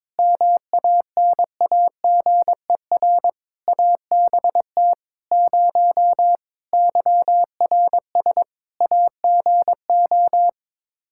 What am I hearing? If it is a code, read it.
Morse code, heard as a keyed tone, about 22 wpm, 700 Hz. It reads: MANAGER ABT 0 YRS AGO
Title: Practice QSO - Exchange 3 - More Info - 22wpm